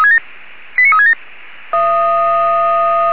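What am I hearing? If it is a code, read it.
Five-tone selective call (ZVEI), heard as quick runs of five tones, tones of about 70 ms. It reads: 89367
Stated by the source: ZVEI five-tone selective call to address 89366, received via SDR.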